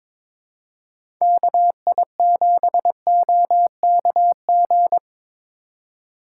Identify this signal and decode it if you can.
Morse code, heard as a keyed tone, about 22 wpm, 700 Hz. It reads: KI7OKG
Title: USA Call Signs - 6 Characters - 3x Character Spacing - 22wpm